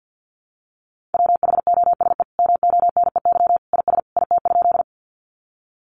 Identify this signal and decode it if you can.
Morse code, heard as a keyed tone, about 49 wpm, 700 Hz. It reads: WHOSE MONEY IS IT?
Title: Sentences from Top 1200 Words 50wpm